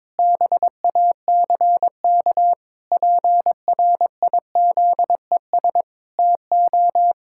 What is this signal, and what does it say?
Morse code, heard as a keyed tone, about 22 wpm, 700 Hz. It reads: BACK PRIZES TO
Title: Pangram Challenge v2 - 22wpm